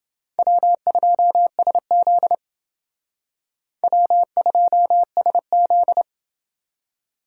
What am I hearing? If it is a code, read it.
Morse code, heard as a keyed tone, about 29 wpm, 700 Hz. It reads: W2SZ W2SZ